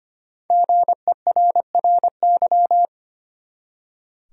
Morse code, heard as a keyed tone, about 25 wpm, 700 Hz. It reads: GERRY
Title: USA Common Names - 25wpm